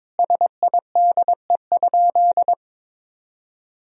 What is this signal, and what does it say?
Morse code, heard as a keyed tone, about 22 wpm, 700 Hz. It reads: SIDE?
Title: Sentences from Top 1000 Words - Review Part 3 - 22wpm